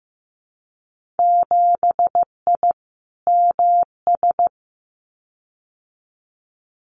Morse code, heard as a keyed tone, about 15 wpm, 700 Hz. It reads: 7I MS
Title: Winter Field Day Exchanges Encore 15wpm